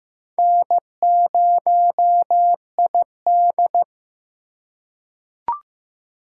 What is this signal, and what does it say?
Morse code, heard as a keyed tone, about 15 wpm, 700 Hz. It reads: N0ID
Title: USA Call Signs - 4 Characters - 4x Character Spacing - 15wpm